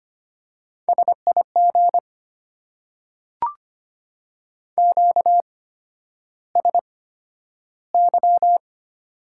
Morse code, heard as a keyed tone, about 25 wpm, spaced as for 11 wpm, 700 Hz. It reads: SIG QSY
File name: Top 90 QSO Elements - 8x Character Spacing - 25wpm